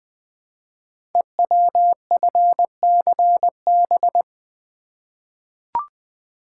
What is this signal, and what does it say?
Morse code, heard as a keyed tone, about 20 wpm, 700 Hz. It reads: EWFCB
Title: Code Groups - 5 Letters - 20wpm